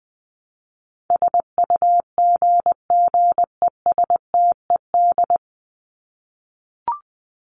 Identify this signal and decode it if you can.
Morse code, heard as a keyed tone, about 20 wpm, 700 Hz. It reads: SUGGESTED
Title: ED Suffix - Top 100 Words with Base Words 20wpm